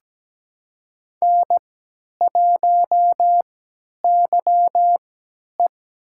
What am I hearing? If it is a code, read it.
Morse code, heard as a keyed tone, about 17 wpm, spaced as for 10 wpm, 700 Hz. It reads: N1YE
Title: USA Call Signs - 4 Characters - 3x Character Spacing - 17wpm